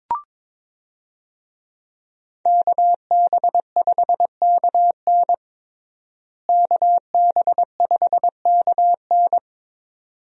Morse code, heard as a keyed tone, about 22 wpm, 700 Hz. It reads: KB5KN KB5KN
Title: USA Call Signs - 5 Characters - Repeated 2x - 22wpm